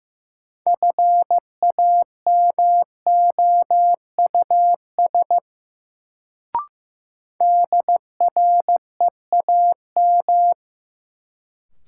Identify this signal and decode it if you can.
Morse code, heard as a keyed tone, about 16 wpm, 700 Hz. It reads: FAMOUS DREAM
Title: Sentences from Top 1500 Words - New Word Drill Alt - 15wpm